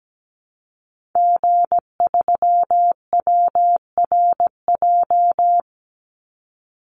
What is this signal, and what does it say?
Morse code, heard as a keyed tone, about 17 wpm, 700 Hz. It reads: G3WRJ
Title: Call Signs - 17wpm